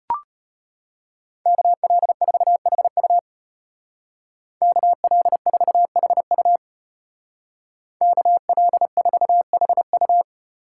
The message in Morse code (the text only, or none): KL4HU KL4HU KL4HU